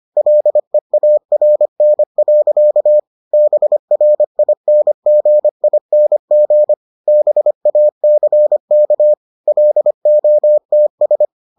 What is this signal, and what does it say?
Morse code, heard as a keyed tone, about 25 wpm, 600 Hz. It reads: LEARN. BRINGING BACK LOTS